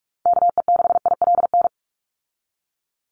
Morse code, heard as a keyed tone, about 44 wpm, 700 Hz. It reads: KE6ILN
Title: USA Call Signs 45wpm